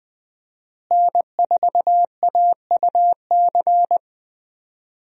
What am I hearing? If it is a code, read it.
Morse code, heard as a keyed tone, about 20 wpm, 700 Hz. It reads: N4AUC
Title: USA Call Signs - 5 Characters - Repeated 2x - 20wpm